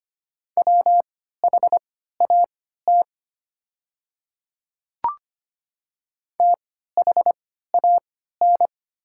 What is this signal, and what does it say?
Morse code, heard as a keyed tone, about 25 wpm, spaced as for 14 wpm, 700 Hz. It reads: WHAT THAN
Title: 3x Character Spacing - Top 100 Words - 25wpm